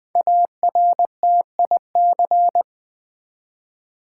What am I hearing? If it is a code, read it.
Morse code, heard as a keyed tone, about 20 wpm, 700 Hz. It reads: ARTIC
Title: Build A Word - 5 Letters At A Time - 10-Letter Words - 20wpm